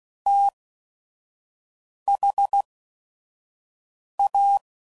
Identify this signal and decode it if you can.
Morse code, heard as a keyed tone, about 16 wpm, 800 Hz. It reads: T H A